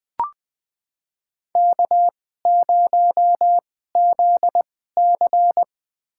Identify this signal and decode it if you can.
Morse code, heard as a keyed tone, about 20 wpm, spaced as for 14 wpm, 700 Hz. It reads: K0ZC
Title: USA Call Signs - 4 Characters - 2x Character Spacing - 20wpm